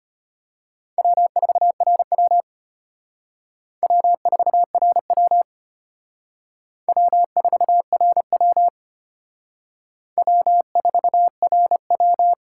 Morse code, heard as a keyed tone, about 33 wpm, 700 Hz. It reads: W4RW W4RW W4RW W4RW